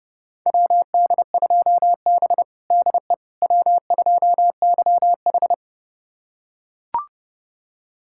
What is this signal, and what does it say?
Morse code, heard as a keyed tone, about 31 wpm, 700 Hz. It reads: WD2B DE W2YH
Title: USA Callsign DE Callsign - 4 Characters - 30wpm